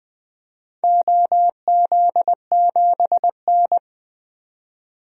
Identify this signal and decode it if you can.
Morse code, heard as a keyed tone, about 20 wpm, 700 Hz. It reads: OZ7N
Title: DXCC-100 Call Signs - 4 Characters - 20wpm